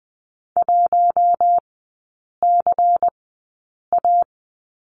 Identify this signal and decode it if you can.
Morse code, heard as a keyed tone, about 20 wpm, spaced as for 8 wpm, 690 Hz. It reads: 1CA
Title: 093 Lesson 16e - Letter c - Three Characters 20wpm